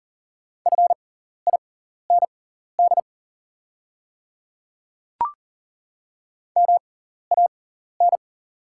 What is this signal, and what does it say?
Morse code, heard as a keyed tone, about 40 wpm, spaced as for 14 wpm, 700 Hz. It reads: FIND MAN
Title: 6x Character Spacing - Top 200 Words - 40wpm